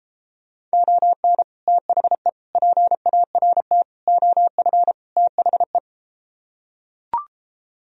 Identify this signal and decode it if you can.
Morse code, heard as a keyed tone, about 32 wpm, 700 Hz. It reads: ON THE PART OF THE